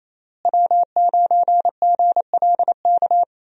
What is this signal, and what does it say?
Morse code, heard as a keyed tone, about 28 wpm, 700 Hz. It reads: W9GLK
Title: USA Call Signs - 5 Characters - Repeated 2x - 28wpm